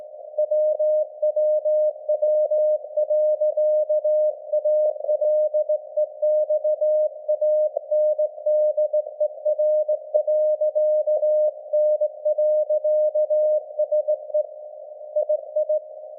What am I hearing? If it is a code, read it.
Morse code, heard as a keyed tone, about 18 wpm, 610 Hz. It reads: WWW.ALEXANDER.N.SE II
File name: laengstwellenempfang / SDRSharp 17kHz AF